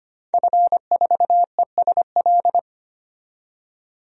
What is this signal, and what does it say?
Morse code, heard as a keyed tone, about 25 wpm, 700 Hz. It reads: F4ESL
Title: DXCC-100 Call Signs - 5 Characters - 25wpm